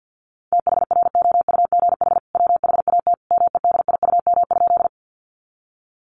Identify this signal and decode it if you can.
Morse code, heard as a keyed tone, about 50 wpm, 700 Hz. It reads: THROUGH WHAT MEDIUM?
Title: Sentences from Top 900 Words 50wpm